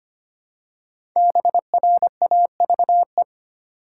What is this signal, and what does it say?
Morse code, heard as a keyed tone, about 25 wpm, 700 Hz. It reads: BRAVE
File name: 5 Letter Words - First 3 Letters Given - 25wpm